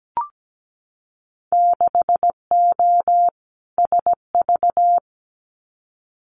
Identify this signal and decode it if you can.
Morse code, heard as a keyed tone, about 17 wpm, 700 Hz. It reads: 6O SV
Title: Winter Field Day Exchanges Encore 17wpm